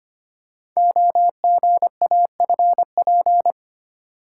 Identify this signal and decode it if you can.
Morse code, heard as a keyed tone, about 25 wpm, 700 Hz. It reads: OGAFP
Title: Code Groups - 5 Letters - 25wpm